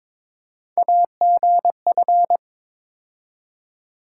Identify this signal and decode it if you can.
Morse code, heard as a keyed tone, about 22 wpm, 700 Hz. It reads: AGF
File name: practice / 22wpm / Three Letter-Number Combinations - 22wpm